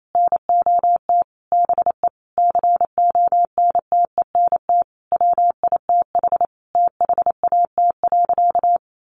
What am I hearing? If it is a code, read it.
Morse code, heard as a keyed tone, about 28 wpm, 700 Hz. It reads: NOT BE CONTENT WITH THAT.
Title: Sentences from Top 900 Words 28wpm